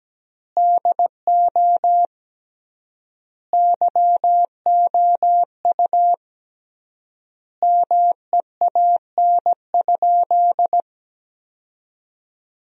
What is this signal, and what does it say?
Morse code, heard as a keyed tone, about 17 wpm, 700 Hz. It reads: DO YOU MEAN?
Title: 3x Word-Spacing - Sentences from Top 500 Words - 17wpm